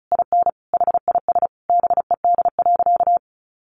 Morse code, heard as a keyed tone, about 34 wpm, 700 Hz. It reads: IN HIS BED.
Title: Sentences from Top 500 Words 35wpm